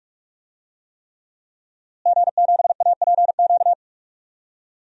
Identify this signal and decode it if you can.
Morse code, heard as a keyed tone, about 46 wpm, 700 Hz. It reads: G7APQ
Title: DXCC-100 Call Signs - 5 Characters - Repeated 3x - 45wpm